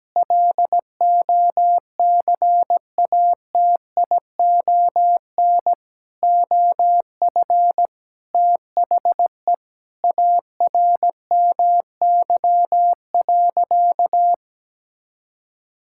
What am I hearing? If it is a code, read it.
Morse code, heard as a keyed tone, about 17 wpm, 700 Hz. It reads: LOCATION OF THE ARMY.
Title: Sentences from Top 1000 Words - Review Part 6 - 17wpm